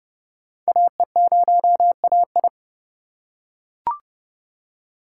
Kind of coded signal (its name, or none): Morse code